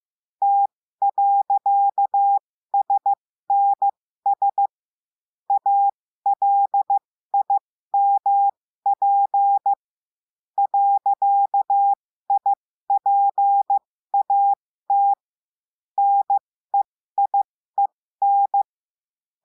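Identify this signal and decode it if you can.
Morse code, heard as a keyed tone, about 15 wpm, spaced as for 13 wpm, 800 Hz. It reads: T.SNS ALIMP .IPAT NEIEN
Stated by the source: Morse lesson adding the period, sending practice five-character groups.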